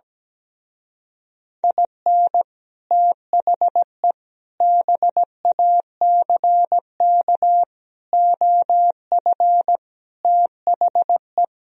Morse code, heard as a keyed tone, about 17 wpm, 700 Hz. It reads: IN THE BACK OF THE
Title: Speed-Racing - Sets of 5 Words 17wpm